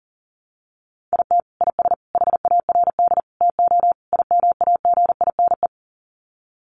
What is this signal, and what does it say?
Morse code, heard as a keyed tone, about 40 wpm, 700 Hz. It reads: IT IS HARD TO IMAGINE
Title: Sets of 5 Words Encore2 40wpm